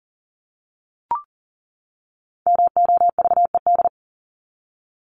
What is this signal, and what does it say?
Morse code, heard as a keyed tone, about 40 wpm, 700 Hz. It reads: MOVED
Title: ED Suffix - Top 100 Words with Base Words 40wpm